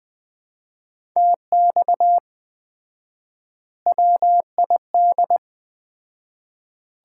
Morse code, heard as a keyed tone, about 19 wpm, 700 Hz. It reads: TX WID
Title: Common QSO Phrases - 4x Word-Spacing - 20wpm